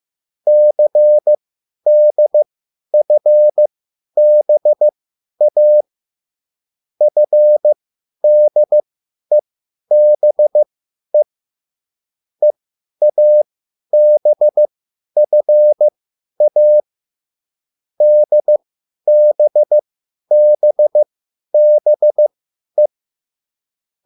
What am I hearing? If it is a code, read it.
Morse code, heard as a keyed tone, about 15 wpm, spaced as for 10 wpm, 600 Hz. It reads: CDFBA FDEBE EABFA DBBBE
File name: cwbook